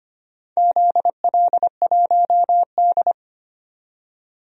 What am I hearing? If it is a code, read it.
Morse code, heard as a keyed tone, about 25 wpm, 700 Hz. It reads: ZL1D
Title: DXCC-100 Call Signs - 4 Characters - 25wpm